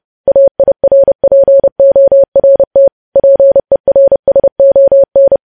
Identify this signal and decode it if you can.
Morse code, heard as a keyed tone, about 30 wpm, 550 Hz. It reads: AIRPORT PERSON
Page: news headlines in morse code at 30 wpm